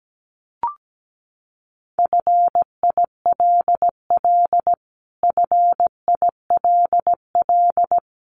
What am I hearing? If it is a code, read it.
Morse code, heard as a keyed tone, about 17 wpm, 700 Hz. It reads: FILL FILL
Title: Top 500 Words Repeated 17wpm